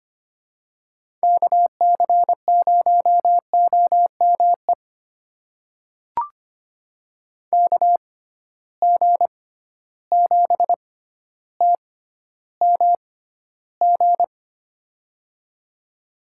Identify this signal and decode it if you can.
Morse code, heard as a keyed tone, about 25 wpm, spaced as for 13 wpm, 700 Hz. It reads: KC0OME KG7TMG
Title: USA Call Signs - 6 Characters - 6x Character Spacing - 25wpm